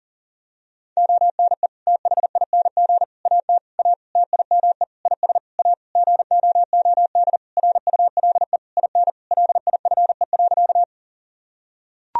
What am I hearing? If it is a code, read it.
Morse code, heard as a keyed tone, about 40 wpm, 700 Hz. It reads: ONE THING AT A TIME IS A GOOD RULE IN LIFE.
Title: Sentences from Top 1000 Words - Review Part 2 - 40wpm